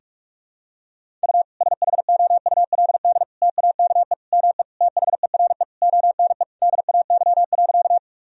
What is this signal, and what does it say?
Morse code, heard as a keyed tone, about 46 wpm, 700 Hz. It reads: U SHOULD TAKE ME THERE ONE DAY.
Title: Sentences from Top 300 Words and Abbreviations 45wpm